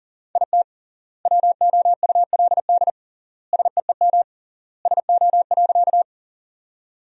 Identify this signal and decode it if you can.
Morse code, heard as a keyed tone, about 40 wpm, 700 Hz. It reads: IT WOULD SEEM SO.
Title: 3x Word-Spacing - Sentences from Top 300 Words - 40wpm